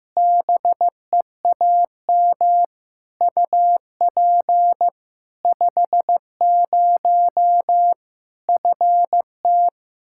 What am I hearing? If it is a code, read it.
Morse code, heard as a keyed tone, about 15 wpm, 700 Hz. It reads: BEAM UP 50 FT
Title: Practice QSO - Exchange 2 - Info - 15wpm